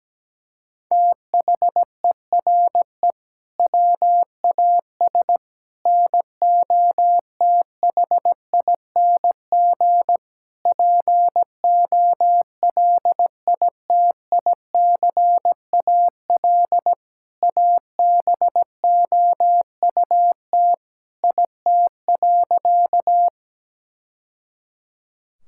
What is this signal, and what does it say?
Morse code, heard as a keyed tone, about 17 wpm, 700 Hz. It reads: THERE WAS NOTHING POLITICAL ABOUT IT.
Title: Sentences from Top 1000 Words - Review Part 5 - 17wpm